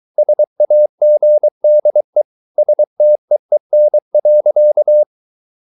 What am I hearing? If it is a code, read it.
Morse code, heard as a keyed tone, about 23 wpm, 600 Hz. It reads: SAGDE STEEN.
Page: Ferie 23 wpm